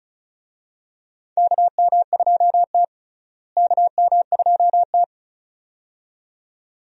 Morse code, heard as a keyed tone, about 36 wpm, 700 Hz. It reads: KM2T KM2T
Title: USA Call Signs - 4 Characters - Repeated 2x - 35wpm